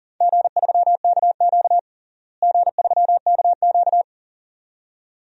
Morse code, heard as a keyed tone, about 40 wpm, 700 Hz. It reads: G3KQ G3KQ
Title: DXCC-100 Call Signs - 4 Characters - Repeated 2x - 40wpm